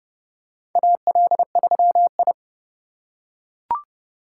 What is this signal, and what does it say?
Morse code, heard as a keyed tone, about 29 wpm, 700 Hz. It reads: AL3I
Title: USA Call Signs - 4 Characters - 3x Character Spacing - 30wpm